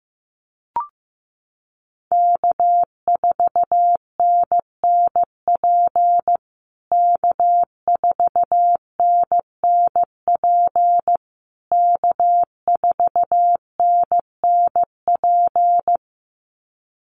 Morse code, heard as a keyed tone, about 14 wpm, 700 Hz. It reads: K4NNP K4NNP K4NNP
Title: USA Call Signs Repeated 3x 15wpm